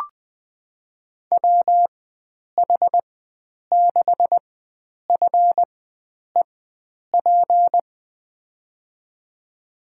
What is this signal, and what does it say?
Morse code, heard as a keyed tone, about 20 wpm, spaced as for 9 wpm, 700 Hz. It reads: WH6FEP
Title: USA Call Signs - 4x Character Spacing - 20wpm